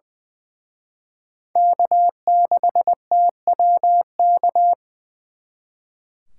Morse code, heard as a keyed tone, about 24 wpm, 700 Hz. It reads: K6TWK
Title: Speed Racing - USA 1x3 Call Signs 20wpm